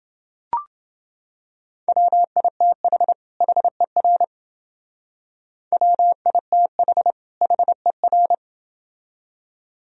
Morse code, heard as a keyed tone, about 28 wpm, 700 Hz. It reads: WITH HER WITH HER